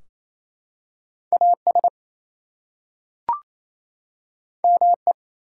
Morse code, heard as a keyed tone, about 27 wpm, 700 Hz. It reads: AS ME
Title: Two Letter Words - 28wpm